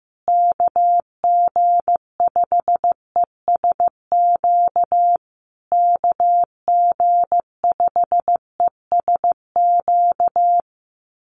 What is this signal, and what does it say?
Morse code, heard as a keyed tone, about 15 wpm, 700 Hz. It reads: KG5ESQ KG5ESQ
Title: USA Call Signs Repeated 3x 15wpm